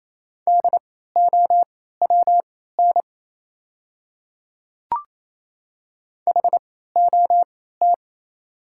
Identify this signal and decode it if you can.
Morse code, heard as a keyed tone, about 28 wpm, spaced as for 16 wpm, 690 Hz. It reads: DOWN HOT
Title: 3x Character Spacing - Top 100 Words - 28wpm